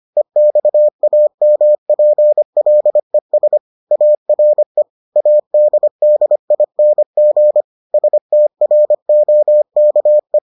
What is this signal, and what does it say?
Morse code, heard as a keyed tone, about 25 wpm, 600 Hz. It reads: EXAMPLES ARE ADDING STROKE